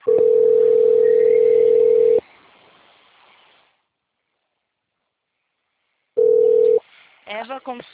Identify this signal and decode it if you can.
Ringback tone (North America), 440 + 480 Hz.